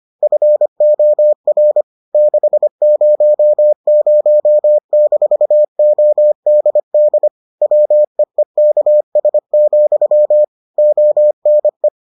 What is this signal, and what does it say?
Morse code, heard as a keyed tone, about 25 wpm, 600 Hz. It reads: FOR 600-ODD WEEKS, ONE